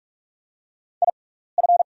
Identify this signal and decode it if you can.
Morse code, heard as a keyed tone, about 44 wpm, spaced as for 15 wpm, 690 Hz. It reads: IF
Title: 6x Character Spacing - Top 100 Words - 45wpm